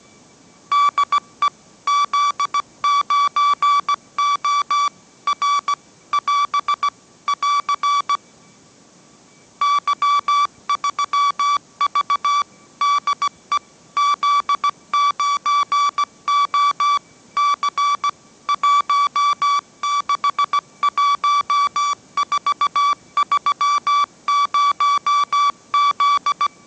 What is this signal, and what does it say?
Morse code, heard as a keyed tone, about 17 wpm, 1150 Hz. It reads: DE Z9O R <AS> <AR> Y3V DE Z9O C 161430Z
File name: mrx / mrx080 18wpm